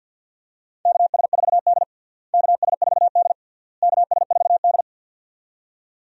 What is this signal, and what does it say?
Morse code, heard as a keyed tone, about 51 wpm, 700 Hz. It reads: KS4D KS4D KS4D
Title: USA Call Signs - 4 Characters - Repeated 3x - 50wpm